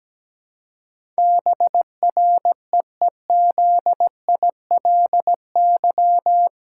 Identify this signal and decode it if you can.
Morse code, heard as a keyed tone, about 17 wpm, 700 Hz. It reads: BREEZILY
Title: Pangram Challenge v2 - 17wpm